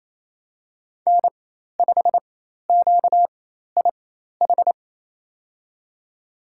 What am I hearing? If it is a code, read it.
Morse code, heard as a keyed tone, about 28 wpm, spaced as for 13 wpm, 700 Hz. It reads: N5QIH
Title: USA Call Signs - 5 Characters - 4x Character Spacing - 28wpm